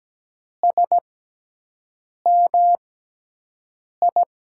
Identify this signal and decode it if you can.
Morse code, heard as a keyed tone, about 17 wpm, spaced as for 6 wpm, 700 Hz. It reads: SMI
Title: USA Call Signs - 6 Characters - 6x Character Spacing - 17wpm